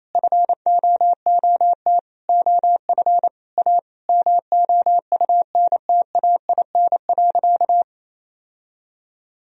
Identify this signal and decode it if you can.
Morse code, heard as a keyed tone, about 27 wpm, 700 Hz. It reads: FOOT OF A MOUNTAIN.
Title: Sentences from Top 1000 Words - Review Part 3 - 28wpm